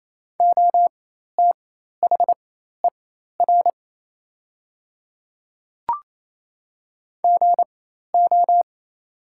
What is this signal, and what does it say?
Morse code, heard as a keyed tone, about 28 wpm, spaced as for 13 wpm, 690 Hz. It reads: OTHER GO
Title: 4x Character Spacing - Top 300 Words - 28wpm